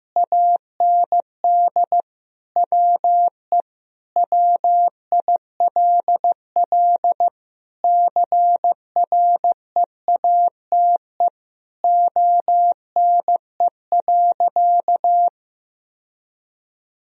Morse code, heard as a keyed tone, about 15 wpm, 700 Hz. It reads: AND WE WILL CREATE ONE.
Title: Sentences from Top 1000 Words - Review Part 3 - 15wpm